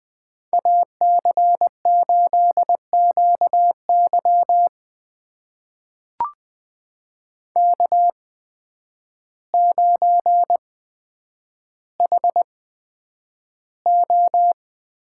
Morse code, heard as a keyed tone, about 20 wpm, spaced as for 9 wpm, 700 Hz. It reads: AC8QY K9HO
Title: USA Call Signs - 8x Character Spacing - 20wpm